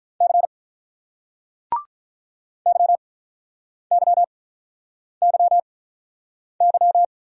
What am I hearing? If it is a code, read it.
Morse code, heard as a keyed tone, about 48 wpm, 700 Hz. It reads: C Y Y Y Y